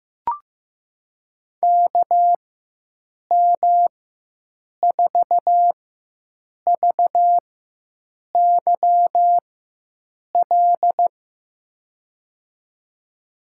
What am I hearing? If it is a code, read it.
Morse code, heard as a keyed tone, about 16 wpm, spaced as for 7 wpm, 700 Hz. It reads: KM4VYL
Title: USA Call Signs - 4x Character Spacing - 15wpm